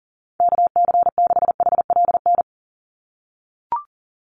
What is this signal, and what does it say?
Morse code, heard as a keyed tone, about 40 wpm, 700 Hz. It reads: KC6HLN
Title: USA Call Signs 40wpm